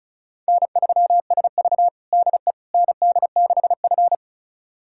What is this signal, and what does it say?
Morse code, heard as a keyed tone, about 36 wpm, 700 Hz. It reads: N3SV DE ND6F
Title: USA Callsign DE Callsign - 4 Characters - 35wpm